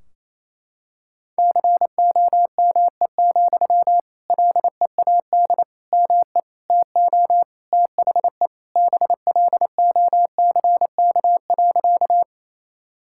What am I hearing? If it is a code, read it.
Morse code, heard as a keyed tone, about 28 wpm, 700 Hz. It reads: COME, LEAD ME TO THE BLOCK.
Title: Sentences from Top 1600 Words 28wpm